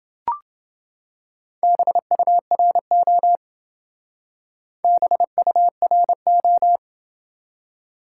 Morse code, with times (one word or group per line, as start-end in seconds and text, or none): 1.6-3.4 BURO
4.8-6.8 BURO